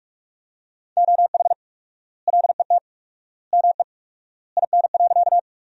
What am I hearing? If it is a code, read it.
Morse code, heard as a keyed tone, about 44 wpm, 700 Hz. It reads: OH LET ME IN.
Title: 4x Word-Spacing - Sentences from Top 500 Words - 45wpm